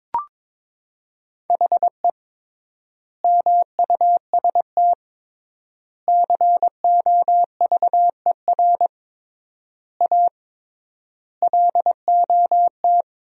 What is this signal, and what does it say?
Morse code, heard as a keyed tone, about 22 wpm, 700 Hz. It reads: HE MUST COVER A LOT
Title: 3x Word-Spacing - Sentences from Top 500 Words - 22wpm